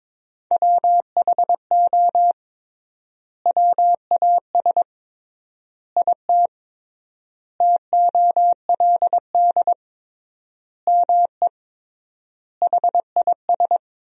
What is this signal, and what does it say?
Morse code, heard as a keyed tone, about 22 wpm, 700 Hz. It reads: WHO WAS IT TOLD ME HIS
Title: 3x Word-Spacing - Sentences from Top 400 Words - 22wpm